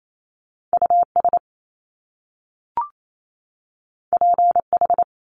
Morse code, heard as a keyed tone, about 28 wpm, 700 Hz. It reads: US PH
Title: Top Two Letter Combinations in Words 28wpm